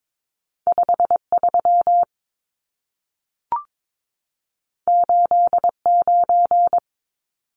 Morse code, heard as a keyed tone, about 22 wpm, 700 Hz. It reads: 53 89
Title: Numbers - 2-Digit 22wpm